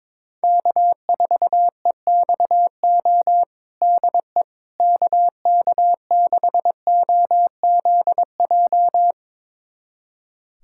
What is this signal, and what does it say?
Morse code, heard as a keyed tone, about 21 wpm, 700 Hz. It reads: K4EXO DE KK6OZJ
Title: USA Callsign DE Callsign 22wpm